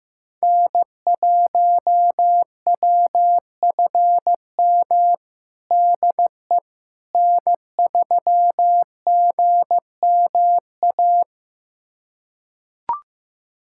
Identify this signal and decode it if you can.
Morse code, heard as a keyed tone, about 15 wpm, 700 Hz. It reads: N1WFM DE N3GMA
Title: USA Callsign DE Callsign - 5 Characters - 15wpm